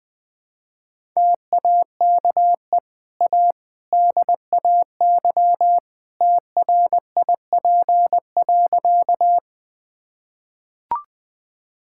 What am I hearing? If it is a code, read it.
Morse code, heard as a keyed tone, about 20 wpm, 700 Hz. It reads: TAKE A DAY TRIP.
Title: Easy Sentences with Hints - 20wpm